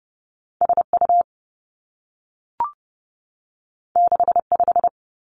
Morse code, heard as a keyed tone, about 30 wpm, 700 Hz. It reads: S U 6 5
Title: Two Challenging Letter-Number Combinations 30wpm